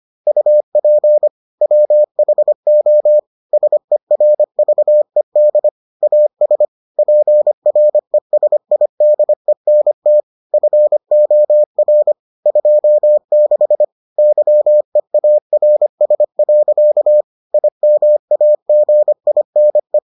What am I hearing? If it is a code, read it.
Morse code, heard as a keyed tone, about 25 wpm, 600 Hz. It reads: UP WHO SERVED AS PRESIDENT FOR 26 YEARS. IMAGINE